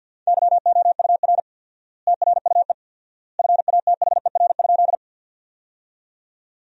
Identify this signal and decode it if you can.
Morse code, heard as a keyed tone, about 50 wpm, 700 Hz. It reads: YOUR TRUE FATHER?
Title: 4x Word-Spacing - Sentences from Top 400 Words - 50wpm